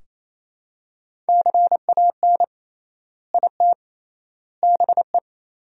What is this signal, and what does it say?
Morse code, heard as a keyed tone, about 28 wpm, 700 Hz. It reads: CAN IT BE
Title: 3x Word-Spacing - Sentences from Top 500 Words - 28wpm